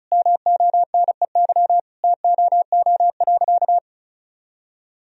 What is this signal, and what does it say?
Morse code, heard as a keyed tone, about 36 wpm, 700 Hz. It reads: MONEY TOO.
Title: Sentences from Top 1000 Words - Review Part 4 - 35wpm